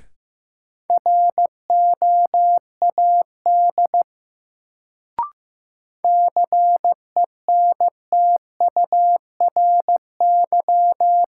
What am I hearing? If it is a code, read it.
Morse code, heard as a keyed tone, about 15 wpm, 700 Hz. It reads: ROAD CENTURY
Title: Sets of 5 Words - New Word Drill - 15wpm